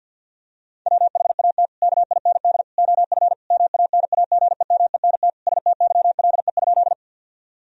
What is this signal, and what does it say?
Morse code, heard as a keyed tone, about 50 wpm, 700 Hz. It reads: WHAT KIND OF MANAGEMENT STYLE?
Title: Sentences from Top 1000 Words - Review Part 4 - 50wpm